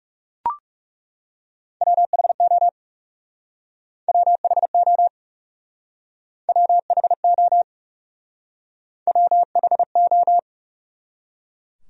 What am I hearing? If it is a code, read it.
Morse code, heard as a keyed tone, about 37 wpm, 700 Hz. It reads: WHO WHO WHO WHO
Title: Speed-Racing Top 500 Words 30wpm